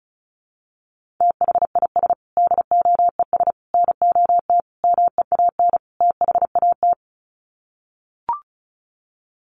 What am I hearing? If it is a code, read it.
Morse code, heard as a keyed tone, about 34 wpm, 700 Hz. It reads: THIS DOES NOT MEAN THAT
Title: Sets of 5 Words Encore2 35wpm